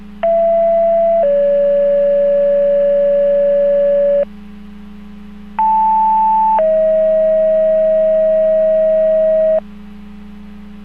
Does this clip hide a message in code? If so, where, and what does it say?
Yes: two-tone page from 0.2 to 9.6 s, heard as a one-second tone, then a three-second tone, about 670 then 570 Hz, about 880 then 640 Hz.